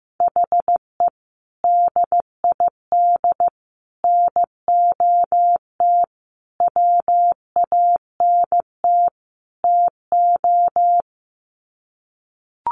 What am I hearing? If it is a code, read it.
Morse code, heard as a keyed tone, about 15 wpm, 700 Hz. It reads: HE DID NOT WANT TO
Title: Sets of 5 Words Encore 15wpm